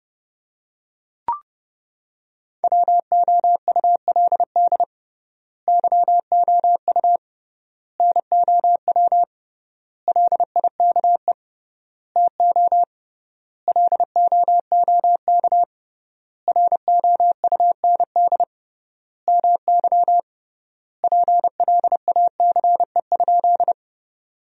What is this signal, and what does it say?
Morse code, heard as a keyed tone, about 30 wpm, 700 Hz. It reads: WOULD YOU NOW LIKE TO LOOK ROUND MY PLACE?
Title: 3x Word-Spacing - Sentences from Top 200 Words - 30wpm